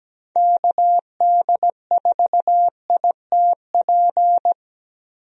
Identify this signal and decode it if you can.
Morse code, heard as a keyed tone, about 17 wpm, 700 Hz. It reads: KD4ITP
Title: USA Call Signs - 6 Characters - Repeated 2x - 17wpm